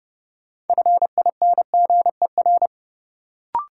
Morse code, heard as a keyed tone, about 30 wpm, 700 Hz. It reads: FINGER
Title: Sentences from Top 1600 Words - New Word Drill - 30wpm